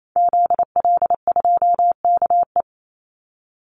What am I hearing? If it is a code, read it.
Morse code, heard as a keyed tone, about 28 wpm, 700 Hz. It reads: ZL2KE
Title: Call Signs - 28wpm